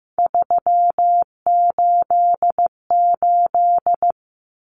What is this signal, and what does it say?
Morse code, heard as a keyed tone, about 15 wpm, 700 Hz. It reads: 388
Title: Common Cut Numbers - 3-Digit 15wpm